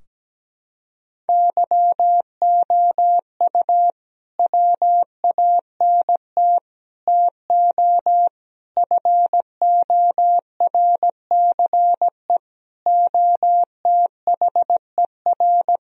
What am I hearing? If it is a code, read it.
Morse code, heard as a keyed tone, about 17 wpm, 700 Hz. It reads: YOU WANT TO FORCE OTHER
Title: Sentences from Top 1000 Words - Review Part 3 - 17wpm